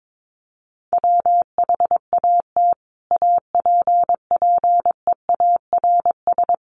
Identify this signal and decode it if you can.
Morse code, heard as a keyed tone, about 22 wpm, 700 Hz. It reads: WHAT APPEARS
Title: Sets of 5 Words Encore2 22wpm